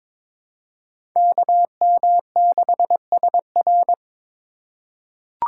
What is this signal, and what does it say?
Morse code, heard as a keyed tone, about 23 wpm, 700 Hz. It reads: KM6SR
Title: USA Call Signs - 5 Characters - 4x Character Spacing - 22wpm